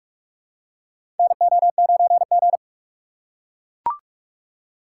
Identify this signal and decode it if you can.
Morse code, heard as a keyed tone, about 42 wpm, 700 Hz. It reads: NO9G
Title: USA Call Signs - 4 Characters - 4x Character Spacing - 45wpm